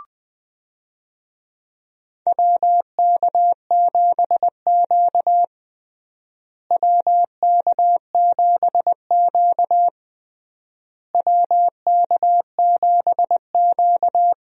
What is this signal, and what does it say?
Morse code, heard as a keyed tone, about 21 wpm, 700 Hz. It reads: WK7Q WK7Q WK7Q
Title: USA Call Signs - 4 Characters - Repeated 3x - 20wpm